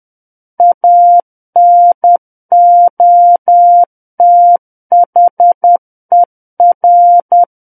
Morse code, heard as a keyed tone, about 10 wpm, 700 Hz. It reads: ANOTHER
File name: Book One / 01-Chapter 1 - 10 WPM